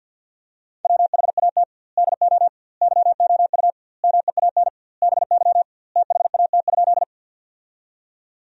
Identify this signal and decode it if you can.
Morse code, heard as a keyed tone, about 50 wpm, 700 Hz. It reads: WHAT DO YOU MEAN BY THAT?
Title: 2x Word-Spacing - Sentences from Top 200 Words - 50wpm